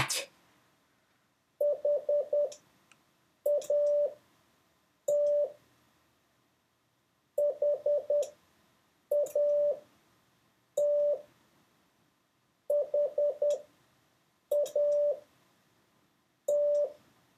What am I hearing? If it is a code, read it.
Morse code, heard as a keyed tone, about 10 wpm, spaced as for 6 wpm, 580 Hz. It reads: HAT HAT HAT